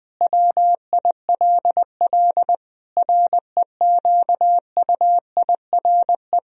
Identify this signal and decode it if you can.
Morse code, heard as a keyed tone, about 20 wpm, 700 Hz. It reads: WILL REQUIRE
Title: Sentences from Top 1000 Words - Review Part 4 - 20wpm